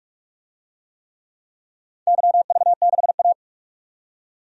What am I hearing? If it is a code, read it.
Morse code, heard as a keyed tone, about 44 wpm, 700 Hz. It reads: YV6A
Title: DXCC-100 Call Signs - 4 Characters - 45wpm